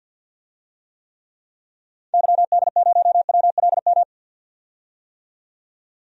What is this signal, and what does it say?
Morse code, heard as a keyed tone, about 51 wpm, 700 Hz. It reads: YD0WLM
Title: DXCC-100 Call Signs - 6 Characters - 50wpm